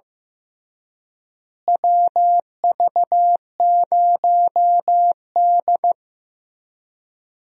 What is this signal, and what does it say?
Morse code, heard as a keyed tone, about 15 wpm, 700 Hz. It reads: WV0D
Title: USA Call Signs - 4 Characters - Repeated 3x - 15wpm